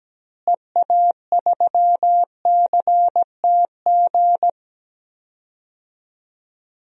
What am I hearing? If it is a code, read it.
Morse code, heard as a keyed tone, about 17 wpm, 700 Hz. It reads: EA3CTG
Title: DXCC-100 Call Signs - 6 Characters - Repeated 3x - 17wpm